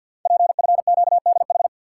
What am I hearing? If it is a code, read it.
Morse code, heard as a keyed tone, about 52 wpm, 700 Hz. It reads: PFQDH
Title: Code Groups - 5 Letters - 50wpm